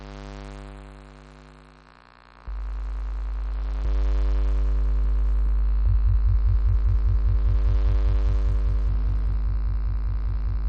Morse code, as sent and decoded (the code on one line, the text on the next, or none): none